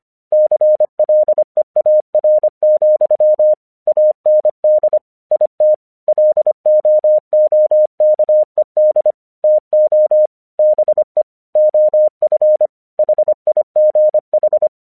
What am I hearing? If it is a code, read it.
Morse code, heard as a keyed tone, about 25 wpm, 610 Hz. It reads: CLEAR, AND IT LOOKED TO BE OF HIGH
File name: 1 English 25wpm / AQC26